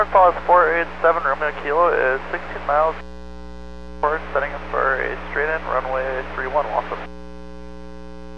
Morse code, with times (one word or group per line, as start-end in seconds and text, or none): none